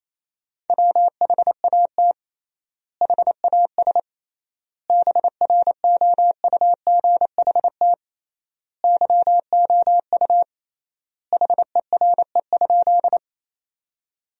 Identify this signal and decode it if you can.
Morse code, heard as a keyed tone, about 28 wpm, 700 Hz. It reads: WHAT HAS BROUGHT YOU HERE?
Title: 3x Word-Spacing - Sentences from Top 500 Words - 28wpm